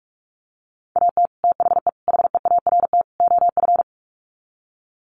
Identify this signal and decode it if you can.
Morse code, heard as a keyed tone, about 44 wpm, 700 Hz. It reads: AT THE HEART OF
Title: Sets of 4 Words 45wpm